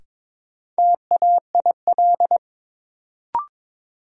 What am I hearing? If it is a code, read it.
Morse code, heard as a keyed tone, about 22 wpm, 700 Hz. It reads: TAIL
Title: Sentences from Top 1000 Words - Review Part 3 - New Word Drill - 22wpm